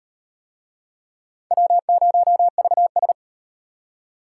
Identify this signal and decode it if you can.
Morse code, heard as a keyed tone, about 33 wpm, 700 Hz. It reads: W0VS